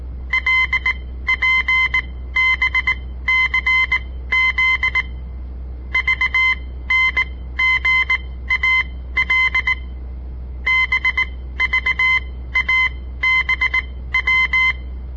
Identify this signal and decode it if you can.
Morse code, heard as a keyed tone, about 19 wpm, spaced as for 14 wpm, 2050 Hz. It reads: LPBCZ VNGAL BVABW
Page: On the air Morse code practice from the Southern Montana Amatuer Radio Association; all code, no read back